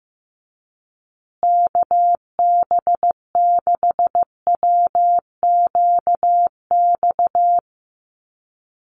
Morse code, heard as a keyed tone, about 15 wpm, 700 Hz. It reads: KB6WQX
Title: USA Call Signs Repeated 3x 15wpm